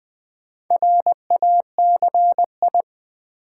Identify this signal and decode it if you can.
Morse code, heard as a keyed tone, about 20 wpm, 700 Hz. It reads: RACI
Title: Build A Word - 2 Letters At A Time - 6-Letter Words - 20wpm